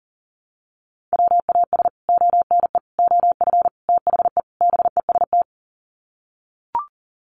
Morse code, heard as a keyed tone, about 40 wpm, 700 Hz. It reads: WAS ONE OF THE BEST
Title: Sets of 5 Words Encore2 40wpm